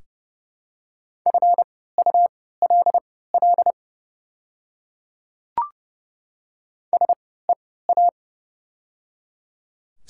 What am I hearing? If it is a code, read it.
Morse code, heard as a keyed tone, about 31 wpm, spaced as for 17 wpm, 700 Hz. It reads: FULL SEA
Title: 3x Character Spacing - Top 500 Words - 30wpm